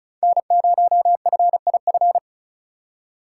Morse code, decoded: N0FIF

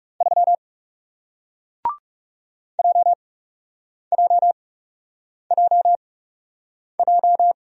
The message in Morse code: 3 JJJJ

